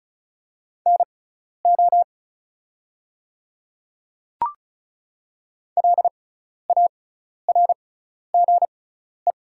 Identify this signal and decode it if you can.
Morse code, heard as a keyed tone, about 34 wpm, spaced as for 12 wpm, 700 Hz. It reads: NO LARGE
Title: 6x Character Spacing - Top 200 Words - 35wpm